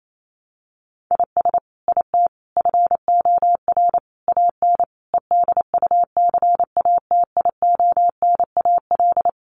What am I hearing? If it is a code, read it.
Morse code, heard as a keyed tone, about 28 wpm, 700 Hz. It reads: IS IT FOR AN EDUCATIONAL
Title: Sentences from Top 1200 Words 28wpm